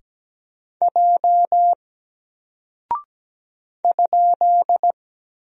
Morse code, heard as a keyed tone, about 17 wpm, 700 Hz. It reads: J ?
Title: Single Character (Letter:Number:Punctuation:Prosign) 17wpm